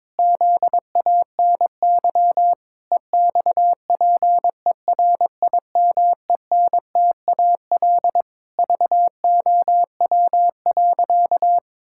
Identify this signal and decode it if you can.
Morse code, heard as a keyed tone, about 22 wpm, 700 Hz. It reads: ZANY EXPERIMENTAL VOW.
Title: Pangram Challenge v2 - 22wpm